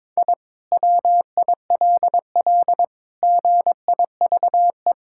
Morse code, heard as a keyed tone, about 22 wpm, 700 Hz. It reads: I WILL GIVE
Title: Sentences from Top 1000 Words - Review Part 6 - 22wpm